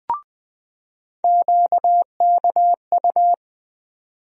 Morse code, heard as a keyed tone, about 20 wpm, 700 Hz. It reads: QKU
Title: USA Call Sign Postfixes - 3 Letters 20wpm